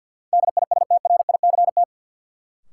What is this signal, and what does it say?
Morse code, heard as a keyed tone, about 53 wpm, 700 Hz. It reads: DISTRICT
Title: Sentences from Top 1600 Words - New Word Drill Alt - 50wpm